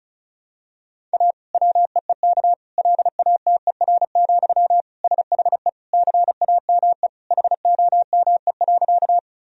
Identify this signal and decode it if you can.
Morse code, heard as a keyed tone, about 36 wpm, 700 Hz. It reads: A WEEK LATER, SHE CAME HOME.
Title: Sentences from Top 1000 Words - Review Part 4 - 35wpm